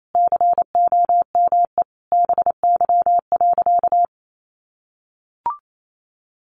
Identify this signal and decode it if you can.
Morse code, heard as a keyed tone, about 28 wpm, 700 Hz. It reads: COME BY.
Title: Sentences from Top 1500 Words 28wpm